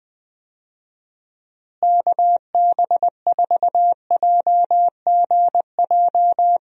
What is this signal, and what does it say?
Morse code, heard as a keyed tone, about 20 wpm, 700 Hz. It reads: KB4JGJ
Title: USA Call Signs - 6 Characters - Repeated 3x - 20wpm